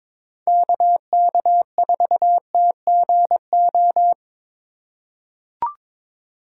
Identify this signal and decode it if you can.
Morse code, heard as a keyed tone, about 23 wpm, 700 Hz. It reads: KK4TGO
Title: USA Call Signs - 4x Character Spacing - 22wpm